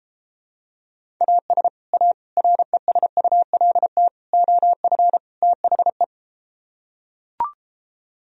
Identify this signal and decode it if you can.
Morse code, heard as a keyed tone, about 33 wpm, 700 Hz. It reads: AS A RESULT OF THE